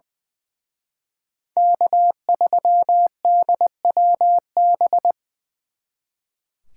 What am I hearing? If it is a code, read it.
Morse code, heard as a keyed tone, about 24 wpm, 700 Hz. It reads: K3DWB
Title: Speed Racing - USA 1x3 Call Signs 20wpm